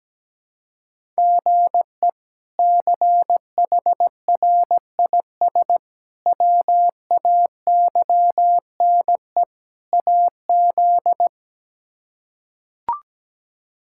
Morse code, heard as a keyed tone, about 17 wpm, 700 Hz. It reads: GE CHRIS WAYNE AZ
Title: K1USN SST QSO Practice - 17wpm